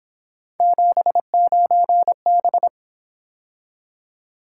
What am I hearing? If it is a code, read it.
Morse code, heard as a keyed tone, about 27 wpm, 700 Hz. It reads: 79B
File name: Speed-Racing - Three Letter-Number Combinations - 22wpm